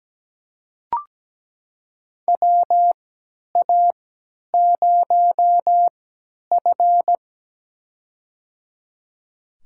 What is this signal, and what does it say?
Morse code, heard as a keyed tone, about 17 wpm, spaced as for 10 wpm, 700 Hz. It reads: WA0F
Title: USA Call Signs - 4 Characters - 3x Character Spacing - 17wpm